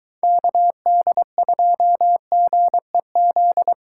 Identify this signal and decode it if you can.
Morse code, heard as a keyed tone, about 22 wpm, 700 Hz. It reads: KD2GEZ